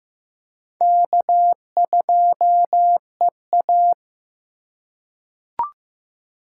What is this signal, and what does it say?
Morse code, heard as a keyed tone, about 15 wpm, 700 Hz. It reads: K2EA
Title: USA Call Signs - 4 Characters - 3x Character Spacing - 15wpm